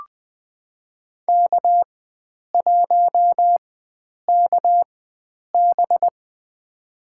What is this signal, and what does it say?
Morse code, heard as a keyed tone, about 20 wpm, spaced as for 9 wpm, 700 Hz. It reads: K1KB
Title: USA Call Signs - 4 Characters - 4x Character Spacing - 20wpm